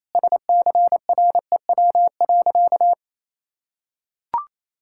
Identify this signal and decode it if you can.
Morse code, heard as a keyed tone, about 28 wpm, 700 Hz. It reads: SCREW.
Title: Sentences from Top 1600 Words 28wpm